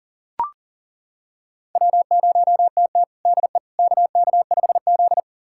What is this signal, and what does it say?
Morse code, heard as a keyed tone, about 40 wpm, 700 Hz. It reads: W0TT DE KK5Z
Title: USA Callsign DE Callsign - 4 Characters - 40wpm